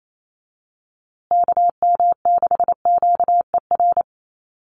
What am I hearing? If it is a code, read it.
Morse code, heard as a keyed tone, about 28 wpm, 690 Hz. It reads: KM6QER
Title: USA Call Signs Repeated 3x 28wpm